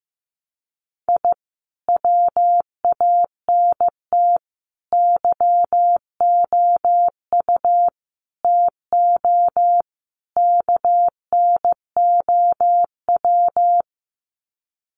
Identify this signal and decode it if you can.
Morse code, heard as a keyed tone, about 15 wpm, 700 Hz. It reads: I WANT YOU TO KNOW
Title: Sets of 5 Words Encore 15wpm